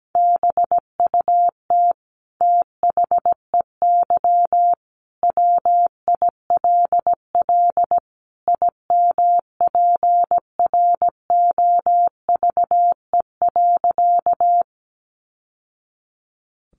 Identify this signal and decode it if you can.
Morse code, heard as a keyed tone, about 17 wpm, 700 Hz. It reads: BUT THEY WILL IMPROVE.
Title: Sentences from Top 800 Words 17wpm